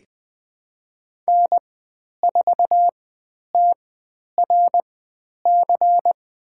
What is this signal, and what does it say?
Morse code, heard as a keyed tone, about 20 wpm, spaced as for 10 wpm, 700 Hz. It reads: N4TRC
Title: 102 Lesson 17g - Letter u - Call Signs 20wpm